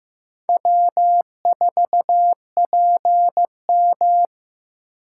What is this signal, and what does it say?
Morse code, heard as a keyed tone, about 15 wpm, 700 Hz. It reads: W4PM
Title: USA Call Signs - 4 Characters - 8x Character Spacing - 15wpm